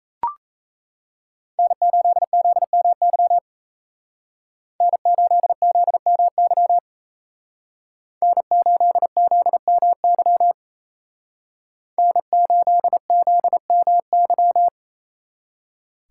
Morse code, heard as a keyed tone, about 36 wpm, 700 Hz. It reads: N8ZMY N8ZMY N8ZMY N8ZMY